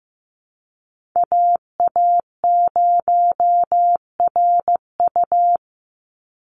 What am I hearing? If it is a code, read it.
Morse code, heard as a keyed tone, about 15 wpm, 700 Hz. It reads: AA0RU
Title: USA Call Signs 15wpm